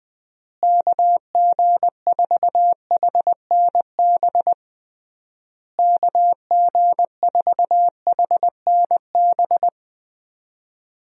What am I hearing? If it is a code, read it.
Morse code, heard as a keyed tone, about 19 wpm, 700 Hz. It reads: KG4HNB KG4HNB
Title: USA Call Signs - 6 Characters - Repeated 3x - 20wpm